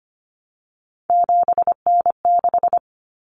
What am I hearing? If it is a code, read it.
Morse code, heard as a keyed tone, about 25 wpm, 700 Hz. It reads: 7N6
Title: Common Cut Numbers - 3-Digit 25wpm